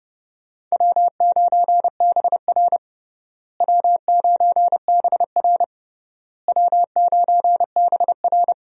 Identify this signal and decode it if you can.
Morse code, heard as a keyed tone, about 30 wpm, 700 Hz. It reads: W9BR W9BR W9BR
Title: USA Call Signs - 4 Characters - Repeated 3x - 30wpm